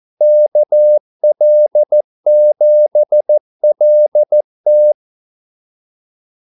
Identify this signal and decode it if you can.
Morse code, heard as a keyed tone, about 14 wpm, 600 Hz. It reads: KL7LT
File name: Callsigns-14wpm